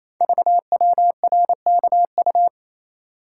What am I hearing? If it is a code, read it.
Morse code, heard as a keyed tone, about 27 wpm, 700 Hz. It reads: VWRKU
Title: Code Groups - 5 Letter:Numbers - 28wpm